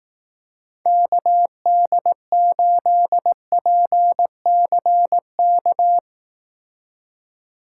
Morse code, heard as a keyed tone, about 18 wpm, 700 Hz. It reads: KD8PCK
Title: Speed Racing - USA 2x3 Call Signs 15wpm